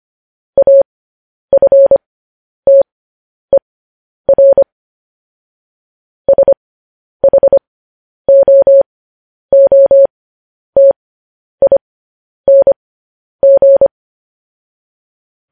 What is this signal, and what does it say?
Morse code, heard as a keyed tone, about 25 wpm, spaced as for 10 wpm, 550 Hz. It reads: AFTER SHOOTING